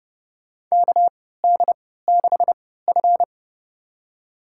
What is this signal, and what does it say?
Morse code, heard as a keyed tone, about 30 wpm, spaced as for 17 wpm, 700 Hz. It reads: KD6F
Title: USA Call Signs - 3x Character Spacing - 30wpm